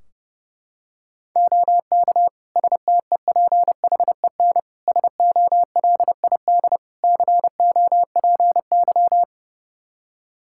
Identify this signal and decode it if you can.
Morse code, heard as a keyed tone, about 30 wpm, 700 Hz. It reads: OK STEPHEN SOLID COPY
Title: K1USN SST QSO Practice 30wpm